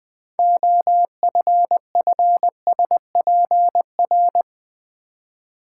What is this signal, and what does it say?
Morse code, heard as a keyed tone, about 20 wpm, 690 Hz. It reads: OFFSPR
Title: Build A Word - 3 Letters At A Time - 9-Letter Words - 20wpm